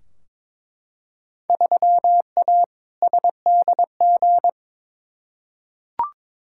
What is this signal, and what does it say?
Morse code, heard as a keyed tone, about 21 wpm, 700 Hz. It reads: 3A SDG
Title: Field Day Exchanges - 22wpm